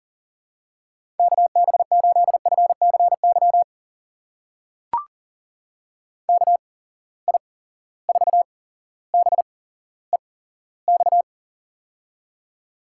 Morse code, heard as a keyed tone, about 40 wpm, spaced as for 17 wpm, 700 Hz. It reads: KB8FCY KI4BEX
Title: USA Call Signs - 6 Characters - 8x Character Spacing - 40wpm